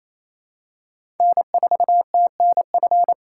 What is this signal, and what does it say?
Morse code, heard as a keyed tone, about 28 wpm, 700 Hz. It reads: N4TNF
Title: USA Call Signs - 4x Character Spacing - 28wpm